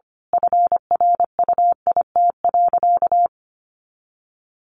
Morse code, heard as a keyed tone, about 25 wpm, 700 Hz. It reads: FRUIT.
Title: Sentences from Top 1300 Words 25wpm